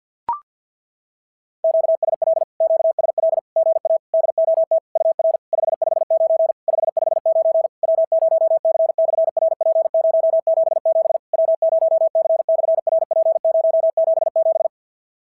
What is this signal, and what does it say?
Morse code, heard as a keyed tone, about 50 wpm, 650 Hz. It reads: QSL QSL GA DOT UR 559 559 W0C/RP077 W0C/RP077
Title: SOTA QSO Practice 50wpm